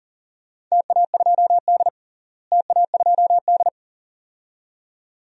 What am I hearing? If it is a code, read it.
Morse code, heard as a keyed tone, about 38 wpm, 700 Hz. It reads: TA2D TA2D
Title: DXCC-100 Call Signs - 4 Characters - Repeated 2x - 40wpm